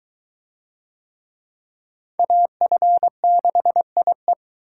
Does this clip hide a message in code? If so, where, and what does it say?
Yes: Morse code from 2.2 to 4.3 s, heard as a keyed tone, about 21 wpm, 700 Hz. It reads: AF6IE